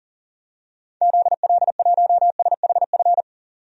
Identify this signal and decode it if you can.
Morse code, heard as a keyed tone, about 40 wpm, 700 Hz. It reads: ZL1SHF
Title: DXCC-100 Call Signs - 6 Characters - 40wpm